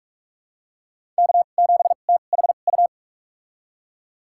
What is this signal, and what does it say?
Morse code, heard as a keyed tone, about 45 wpm, spaced as for 32 wpm, 700 Hz. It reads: K7THU
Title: USA Call Signs - 5 Characters - 2x Character Spacing - 45wpm